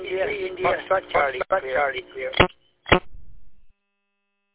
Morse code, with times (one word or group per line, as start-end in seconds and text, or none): none